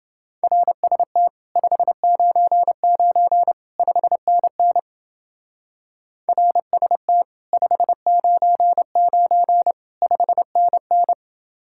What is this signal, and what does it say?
Morse code, heard as a keyed tone, about 29 wpm, 700 Hz. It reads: RST 599 5NN RST 599 5NN